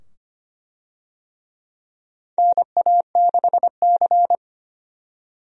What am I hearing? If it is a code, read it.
Morse code, heard as a keyed tone, about 25 wpm, 700 Hz. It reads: NA6C
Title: USA Call Signs - 4 Characters - 25wpm